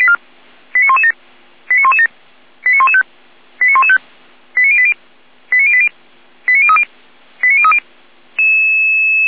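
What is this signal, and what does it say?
Five-tone selective call (ZVEI), heard as quick runs of five tones, tones of about 80 ms. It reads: 89117 89117 89115 89115 89080 89080 89030 89030
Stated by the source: ZVEI five-tone selective call received on an SDR.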